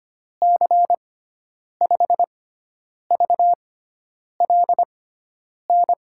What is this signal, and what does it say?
Morse code, heard as a keyed tone, about 25 wpm, spaced as for 9 wpm, 700 Hz. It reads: C5VLN
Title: USA Call Signs - 6 Characters - 6x Character Spacing - 25wpm